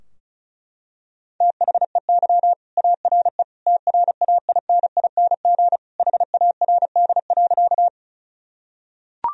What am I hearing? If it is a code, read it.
Morse code, heard as a keyed tone, about 34 wpm, 700 Hz. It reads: THEY ARE TRAINING HARD.
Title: Sentences from Top 1000 Words - Review Part 4 - 35wpm